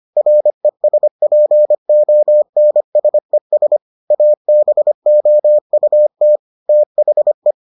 Morse code, heard as a keyed tone, about 25 wpm, 600 Hz. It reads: RESPONSES ABOUT THE